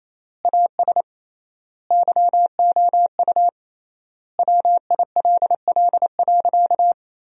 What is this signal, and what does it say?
Morse code, heard as a keyed tone, about 28 wpm, 700 Hz. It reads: AS YOU WILL.
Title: 3x Word-Spacing - Sentences from Top 100 Words - 28wpm